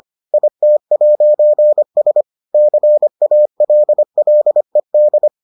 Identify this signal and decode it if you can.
Morse code, heard as a keyed tone, about 25 wpm, 600 Hz. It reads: IT'S CALLED